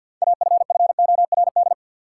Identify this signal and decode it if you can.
Morse code, heard as a keyed tone, about 52 wpm, 700 Hz. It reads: AFFORD
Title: Six Letter Words - 50wpm